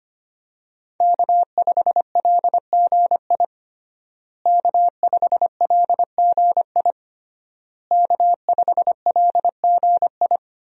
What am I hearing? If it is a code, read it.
Morse code, heard as a keyed tone, about 25 wpm, 700 Hz. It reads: K5LGI K5LGI K5LGI
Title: USA Call Signs - 5 Characters - Repeated 3x - 25wpm